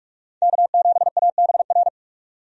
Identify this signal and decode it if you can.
Morse code, heard as a keyed tone, about 46 wpm, 700 Hz. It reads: K7ABR
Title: USA Call Signs - 5 Characters - 45wpm